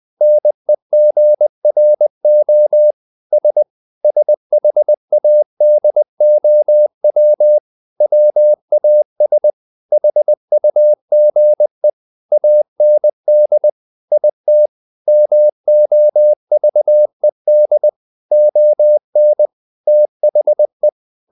Morse code, heard as a keyed tone, about 20 wpm, 600 Hz. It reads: NEGRO S SHADOW WAS HUGE AND IT MOVED ON THE
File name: the old man and the sea part 2 20wpm